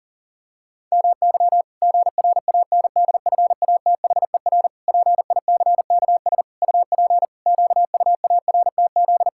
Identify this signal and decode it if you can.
Morse code, heard as a keyed tone, about 40 wpm, 700 Hz. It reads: MY GRANDFATHER PICKS UP QUARTZ
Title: Pangram Challenge v2 - 40wpm